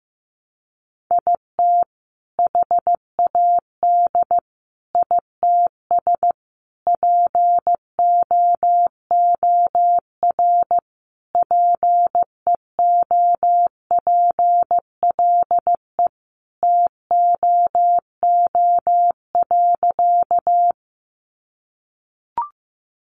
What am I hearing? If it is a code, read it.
Morse code, heard as a keyed tone, about 15 wpm, 700 Hz. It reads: IT HAD ITS POOR PEOPLE TOO.
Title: Sentences from Top 1000 Words 15wpm